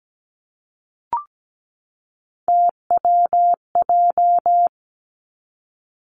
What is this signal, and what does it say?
Morse code, heard as a keyed tone, about 17 wpm, 700 Hz. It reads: TWJ
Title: Three Challenging Letter-Number Combinations 17wpm